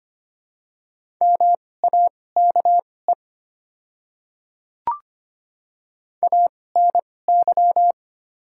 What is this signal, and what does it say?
Morse code, heard as a keyed tone, about 25 wpm, spaced as for 18 wpm, 700 Hz. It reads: MAKE ANY
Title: 2x Character Spacing - Top 200 Words - 25wpm